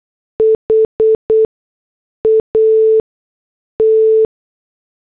Morse code, heard as a keyed tone, about 8 wpm, spaced as for 6 wpm, 440 Hz. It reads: HAT